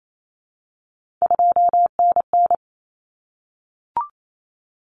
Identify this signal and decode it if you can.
Morse code, heard as a keyed tone, about 28 wpm, 700 Hz. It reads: 2NN
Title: Common Cut Numbers - 3-Digit 28wpm